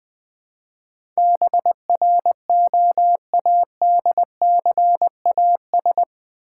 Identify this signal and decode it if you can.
Morse code, heard as a keyed tone, about 20 wpm, 700 Hz. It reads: BROADCAS
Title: Build A Word - 4 Letters At A Time - 11-Letter Words - 20wpm